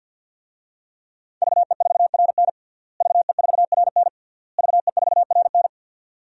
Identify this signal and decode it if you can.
Morse code, heard as a keyed tone, about 48 wpm, 700 Hz. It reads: VE4RN VE4RN VE4RN
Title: DXCC-100 Call Signs - 5 Characters - Repeated 3x - 50wpm